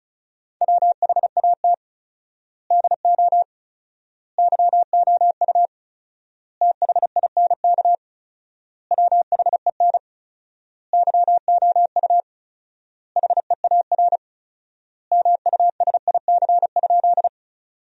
Morse code, heard as a keyed tone, about 36 wpm, 700 Hz. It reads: WHAT DO YOU THINK WHEN YOU HEAR MUSIC?
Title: 4x Word-Spacing - Sentences from Top 300 Words - 35wpm